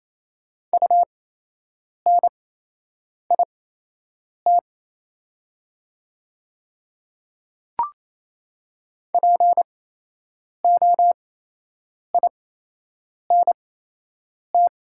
Morse code, heard as a keyed tone, about 28 wpm, spaced as for 8 wpm, 700 Hz. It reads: UNIT POINT
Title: 8x Character Spacing - Top 500 Words - 28wpm